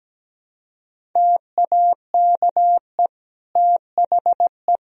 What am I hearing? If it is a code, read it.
Morse code, heard as a keyed tone, about 17 wpm, 700 Hz. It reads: TAKE THE
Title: Sentences from Top 400 Words and Abbreviations 17wpm